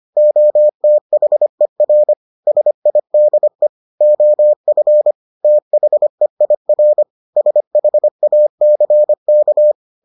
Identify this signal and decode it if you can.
Morse code, heard as a keyed tone, about 25 wpm, 600 Hz. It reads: OTHER SIDE OF THEIR SHACK